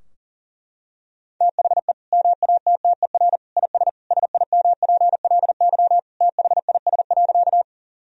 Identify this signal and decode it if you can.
Morse code, heard as a keyed tone, about 40 wpm, 690 Hz. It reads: THE MATTER IS SIMPLY THIS.
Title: Sentences from Top 1000 Words - Review Part 4 - 40wpm